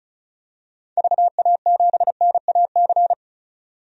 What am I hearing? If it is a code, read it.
Morse code, heard as a keyed tone, about 36 wpm, 690 Hz. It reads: VA7NAC
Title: DXCC-100 Call Signs - 6 Characters - 35wpm